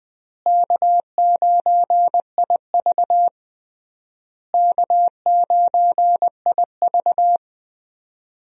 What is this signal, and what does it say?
Morse code, heard as a keyed tone, about 20 wpm, 700 Hz. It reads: K9IV K9IV
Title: USA Call Signs - 4 Characters - Repeated 3x - 20wpm